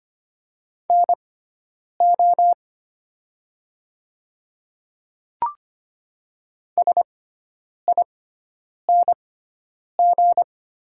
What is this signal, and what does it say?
Morse code, heard as a keyed tone, about 25 wpm, spaced as for 9 wpm, 700 Hz. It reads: NO SING
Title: 6x Character Spacing - Top 500 Words - 25wpm